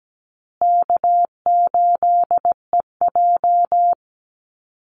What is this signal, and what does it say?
Morse code, heard as a keyed tone, about 17 wpm, 700 Hz. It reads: K8EJ
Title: Call Signs Repeated 3x - 17wpm